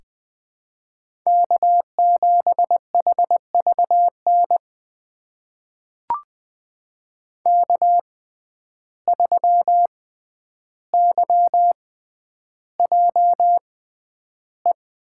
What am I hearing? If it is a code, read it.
Morse code, heard as a keyed tone, about 20 wpm, spaced as for 10 wpm, 700 Hz. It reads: K7HVN K3YJE
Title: USA Call Signs - 5 Characters - 6x Character Spacing - 20wpm